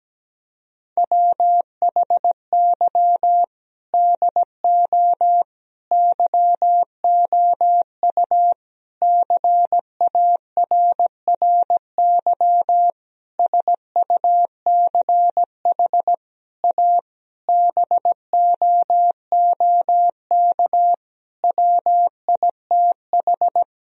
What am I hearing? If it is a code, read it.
Morse code, heard as a keyed tone, about 17 wpm, 700 Hz. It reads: WHY DO YOU CARRY SUCH A BOOK WITH
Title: Sentences from Top 1000 Words - Review Part 1 - 17wpm